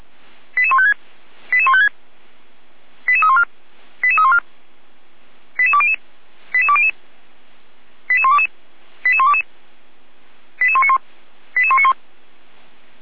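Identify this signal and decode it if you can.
Five-tone selective call (ZVEI), heard as quick runs of five tones, tones of about 70 ms. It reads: 88156 88156 88314 88314 88200 88200 88122 88122 88191 88191
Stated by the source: ZVEI five-tone selective call received on SDR.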